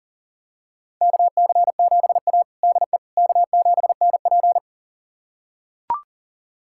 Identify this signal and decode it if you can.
Morse code, heard as a keyed tone, about 40 wpm, 700 Hz. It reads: KC7A DE K7NP
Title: USA Callsign DE Callsign - 4 Characters - 40wpm